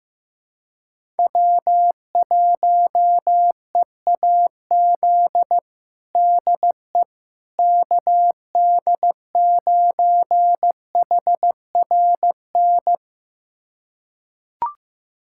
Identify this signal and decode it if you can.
Morse code, heard as a keyed tone, about 15 wpm, 700 Hz. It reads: W1EAZ DE KD9HRN
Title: USA Callsign DE Callsign 15wpm